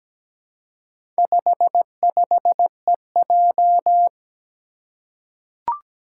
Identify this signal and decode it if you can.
Morse code, heard as a keyed tone, about 17 wpm, 700 Hz. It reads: 55EJ
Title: Four Challenging Letter-Number Combinations 17wpm